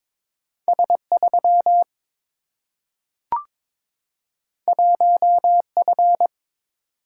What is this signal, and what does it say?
Morse code, heard as a keyed tone, about 22 wpm, 700 Hz. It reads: S 3 1 F
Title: Single-Letter Single-Number 22wpm